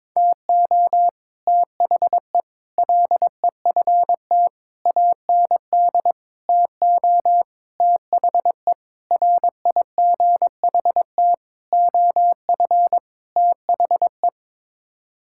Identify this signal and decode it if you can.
Morse code, heard as a keyed tone, about 22 wpm, 700 Hz. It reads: TO THE LEFT AND TO THE RIGHT OF THE
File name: Sets of 9 Words - 22wpm